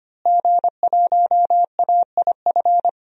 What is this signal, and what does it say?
Morse code, heard as a keyed tone, about 25 wpm, 700 Hz. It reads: G1AIF
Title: DXCC-100 Call Signs - 5 Characters - Repeated 3x - 25wpm